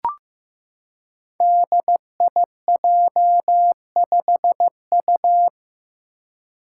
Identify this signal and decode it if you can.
Morse code, heard as a keyed tone, about 15 wpm, 700 Hz. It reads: DIJ5U
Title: Five Challenging Letter-Number Combinations 15wpm